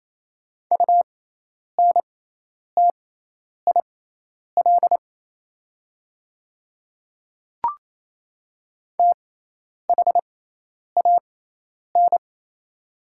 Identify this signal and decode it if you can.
Morse code, heard as a keyed tone, about 27 wpm, spaced as for 10 wpm, 700 Hz. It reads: UNTIL THAN
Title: 6x Character Spacing - Top 300 Words - 28wpm